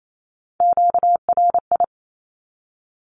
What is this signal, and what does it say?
Morse code, heard as a keyed tone, about 28 wpm, 700 Hz. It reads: QRI
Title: Q-Codes 28wpm